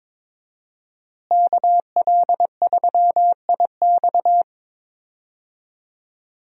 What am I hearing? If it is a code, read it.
Morse code, heard as a keyed tone, about 22 wpm, 700 Hz. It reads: KL3IX
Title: USA Call Signs - 5 Characters - Repeated 3x - 22wpm